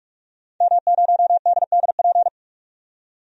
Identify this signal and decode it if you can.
Morse code, heard as a keyed tone, about 46 wpm, 700 Hz. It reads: M0DDP
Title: DXCC-100 Call Signs - 5 Characters - 45wpm